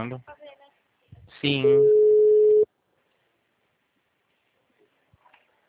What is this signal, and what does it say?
Ringback tone (Europe, 425 Hz), with speech.